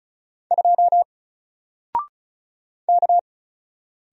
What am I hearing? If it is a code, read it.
Morse code, heard as a keyed tone, about 35 wpm, 700 Hz. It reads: 2 K
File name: Single-Letter-Number-35wpm